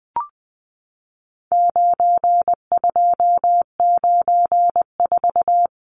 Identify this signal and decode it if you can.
Morse code, heard as a keyed tone, about 20 wpm, 700 Hz. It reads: 9294
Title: Numbers - 4-Digit 20wpm